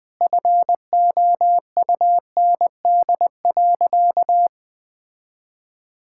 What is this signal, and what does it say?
Morse code, heard as a keyed tone, about 20 wpm, 700 Hz. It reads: FOUND.
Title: Sentences with 9-Word Phrases - 20wpm